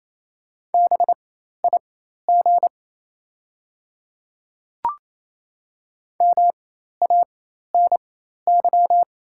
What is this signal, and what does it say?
Morse code, heard as a keyed tone, about 28 wpm, spaced as for 13 wpm, 700 Hz. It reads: BIG MANY
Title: 4x Character Spacing - Top 200 Words - 28wpm